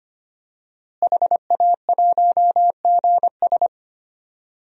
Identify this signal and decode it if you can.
Morse code, heard as a keyed tone, about 25 wpm, 700 Hz. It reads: HA1GS
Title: DXCC-100 Call Signs - 5 Characters - Repeated 2x - 25wpm